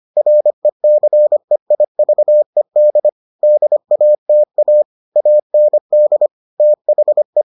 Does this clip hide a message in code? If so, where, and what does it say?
Yes: Morse code from 0.2 to 7.4 s, heard as a keyed tone, about 25 wpm, 600 Hz. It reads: RECEIVED DATA AND THE